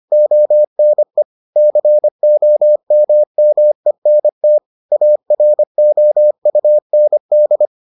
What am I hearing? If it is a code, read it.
Morse code, heard as a keyed tone, about 25 wpm, 600 Hz. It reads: ONE COMMENT AROUND